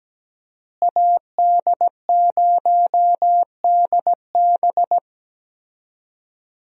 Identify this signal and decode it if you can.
Morse code, heard as a keyed tone, about 17 wpm, 700 Hz. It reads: AD0DB
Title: USA Call Signs - 5 Characters - Repeated 2x - 17wpm